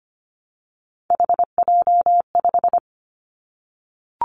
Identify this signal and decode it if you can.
Morse code, heard as a keyed tone, about 25 wpm, 700 Hz. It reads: HJ5
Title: Callsign Prefixes for Top 100 DXCC - 25wpm